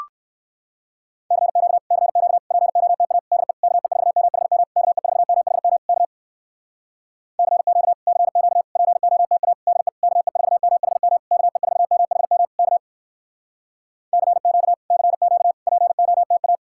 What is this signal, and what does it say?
Morse code, heard as a keyed tone, about 58 wpm, 700 Hz. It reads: CQ CQ POTA DE KE4MHM KE4MHM K CQ CQ POTA DE KE4MHM KE4MHM K CQ CQ POTA